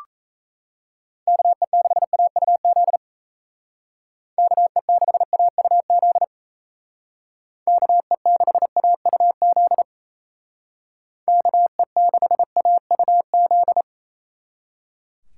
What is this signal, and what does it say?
Morse code, heard as a keyed tone, about 35 wpm, 700 Hz. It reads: KE6AUZ KE6AUZ KE6AUZ KE6AUZ
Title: Speed Racing - USA 2x3 Call Signs 28wpm